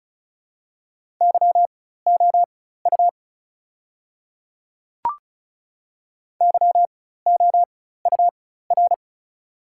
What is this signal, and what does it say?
Morse code, heard as a keyed tone, about 34 wpm, spaced as for 16 wpm, 690 Hz. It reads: YOU YOUR
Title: 4x Character Spacing - Top 500 Words - 35wpm